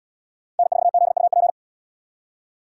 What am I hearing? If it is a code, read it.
Morse code, heard as a keyed tone, about 70 wpm, 700 Hz. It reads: N4ZUP